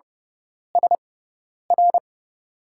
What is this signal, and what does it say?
Morse code, heard as a keyed tone, about 30 wpm, spaced as for 10 wpm, 700 Hz. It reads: SR
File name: Lesson 17g - Letter u - Call Signs 30wpm